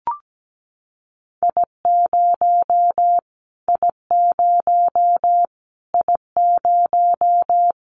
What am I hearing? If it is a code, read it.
Morse code, heard as a keyed tone, about 17 wpm, 700 Hz. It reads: I0 I0 I0
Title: Callsign Prefixes for Top 25 DXCC - Repeated 3x - 17wpm